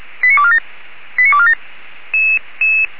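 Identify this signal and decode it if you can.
Five-tone selective call (ZVEI), heard as quick runs of five tones, tones of about 70 ms. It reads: 89247 89247
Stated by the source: ZVEI five-tone selective call 89300 received by SDR.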